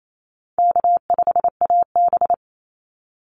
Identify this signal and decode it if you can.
Morse code, heard as a keyed tone, about 28 wpm, 700 Hz. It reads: K5AB
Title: Easy Call Signs Repeated 2x - 28wpm